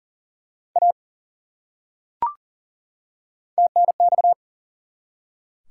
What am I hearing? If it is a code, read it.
Morse code, heard as a keyed tone, about 39 wpm, 700 Hz. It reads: A TNX
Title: Top 180 QSO Elements 40wpm